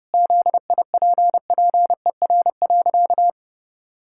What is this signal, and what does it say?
Morse code, heard as a keyed tone, about 28 wpm, 700 Hz. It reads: ZIPPER.
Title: Pangram Challenge v2 - 30wpm